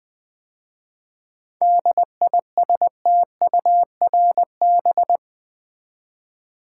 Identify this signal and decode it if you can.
Morse code, heard as a keyed tone, about 20 wpm, 700 Hz. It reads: DISTURB
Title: 7 Letter Words - First 3 Letters Given - 20wpm